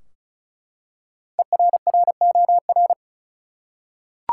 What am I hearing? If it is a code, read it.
Morse code, heard as a keyed tone, about 36 wpm, 700 Hz. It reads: ERROR
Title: Five Letter Words - 35wpm